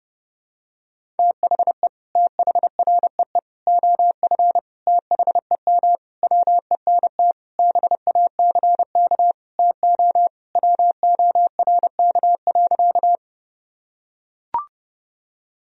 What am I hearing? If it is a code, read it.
Morse code, heard as a keyed tone, about 30 wpm, 700 Hz. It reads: THE THREE OF THEM WENT BACK TO WORK.
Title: Sentences from Top 1000 Words - Review Part 1 - 30wpm